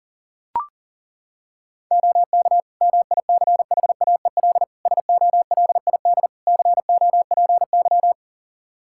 Morse code, heard as a keyed tone, about 40 wpm, 700 Hz. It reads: OK MICHAEL SOLID COPY
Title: K1USN SST QSO Practice 40wpm